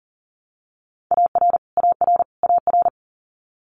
Morse code, heard as a keyed tone, about 41 wpm, 700 Hz. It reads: AR AR AR
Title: Multi-letter Phonemes Repeated 3x 40wpm